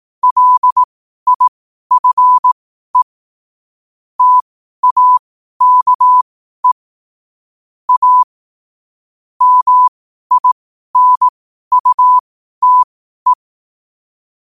Morse code, heard as a keyed tone, about 18 wpm, spaced as for 12 wpm, 1000 Hz. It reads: LIFE TAKE A MINUTE
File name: qotdfiles / 12wpm